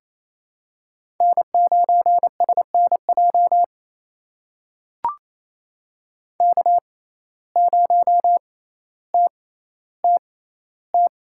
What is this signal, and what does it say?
Morse code, heard as a keyed tone, about 27 wpm, spaced as for 14 wpm, 700 Hz. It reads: N9SNJ K0TTT
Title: USA Call Signs - 5 Characters - 6x Character Spacing - 28wpm